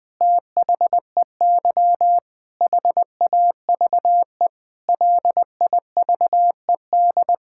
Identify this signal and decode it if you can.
Morse code, heard as a keyed tone, about 21 wpm, 700 Hz. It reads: THEY HAVE LIVED
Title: Sentences with 7-Word Phrases - 20wpm